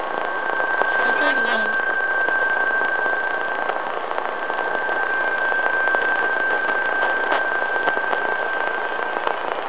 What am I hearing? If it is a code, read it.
PSK31, about 31.25 baud.